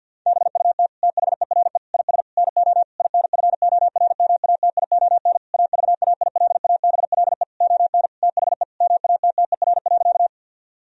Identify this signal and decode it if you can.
Morse code, heard as a keyed tone, about 50 wpm, 700 Hz. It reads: BUT THERE IS NO INFORMATION AVAILABLE ON THE MATTER.
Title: Sentences from Top 1000 Words - Review Part 3 - 50wpm